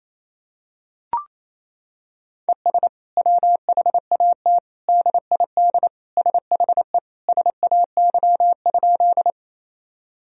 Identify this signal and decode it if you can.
Morse code, heard as a keyed tone, about 28 wpm, 700 Hz. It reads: ES WHAT DID SHE SAY?
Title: Sentences from Top 200 Words and Abbreviations 28wpm